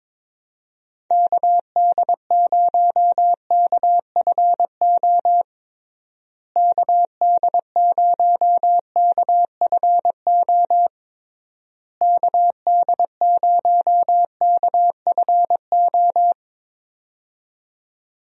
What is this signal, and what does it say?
Morse code, heard as a keyed tone, about 22 wpm, 700 Hz. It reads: KD0KFO KD0KFO KD0KFO
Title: USA Call Signs - 6 Characters - Repeated 3x - 22wpm